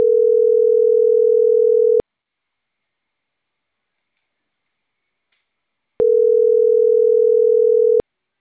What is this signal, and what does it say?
Ringback tone (North America), 440 + 480 Hz.